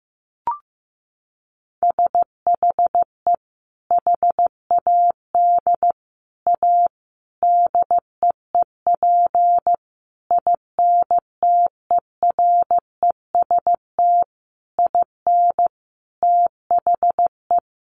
Morse code, heard as a keyed tone, about 15 wpm, 700 Hz. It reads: SHE HAD A DEEP INTEREST IN THE
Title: Sentences from Top 1300 Words 15wpm